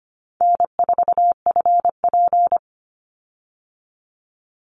Morse code, heard as a keyed tone, about 25 wpm, 700 Hz. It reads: N4FP
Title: Call Signs Repeated 2x - 25wpm